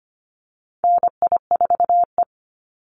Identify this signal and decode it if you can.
Morse code, heard as a keyed tone, about 25 wpm, 700 Hz. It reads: NI4E
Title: Call Signs - 25wpm